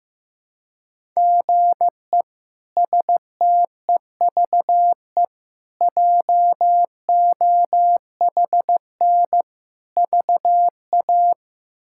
Morse code, heard as a keyed tone, about 15 wpm, 700 Hz. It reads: GE STEVE JOHN VA
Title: K1USN SST QSO Practice - 15wpm